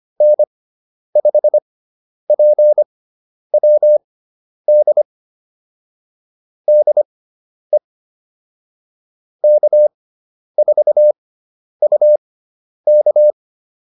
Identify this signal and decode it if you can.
Morse code, heard as a keyed tone, about 25 wpm, spaced as for 10 wpm, 600 Hz. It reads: N5PWD DE K4UK
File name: QSO208 10